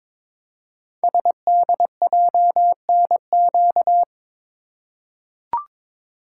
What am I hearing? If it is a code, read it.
Morse code, heard as a keyed tone, about 22 wpm, 700 Hz. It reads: SDJNQ
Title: Code Groups - 5 Letter:Numbers - 22wpm